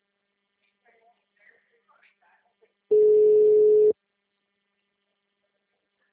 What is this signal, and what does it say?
Ringback tone (Europe, 425 Hz), with speech.